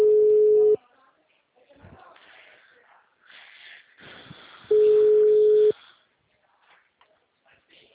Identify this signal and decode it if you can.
Ringback tone (Europe, 425 Hz), with speech.